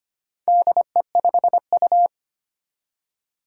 Morse code, heard as a keyed tone, about 25 wpm, 700 Hz. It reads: DE5U
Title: Four Challenging Letter-Number Combinations 25wpm